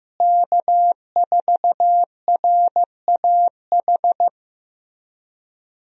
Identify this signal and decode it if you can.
Morse code, heard as a keyed tone, about 15 wpm, 700 Hz. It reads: K4RAH
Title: USA Call Signs - 5 Characters - Repeated 3x - 15wpm